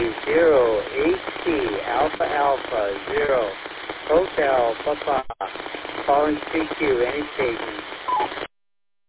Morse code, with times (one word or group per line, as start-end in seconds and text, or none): none